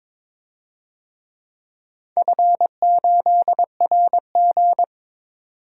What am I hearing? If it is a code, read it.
Morse code, heard as a keyed tone, about 22 wpm, 690 Hz. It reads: F8RG
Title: DXCC-100 Call Signs - 4 Characters - 22wpm